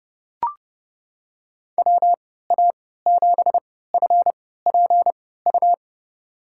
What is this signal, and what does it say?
Morse code, heard as a keyed tone, about 30 wpm, spaced as for 17 wpm, 700 Hz. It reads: WA7FPU
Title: USA Call Signs - 6 Characters - 3x Character Spacing - 30wpm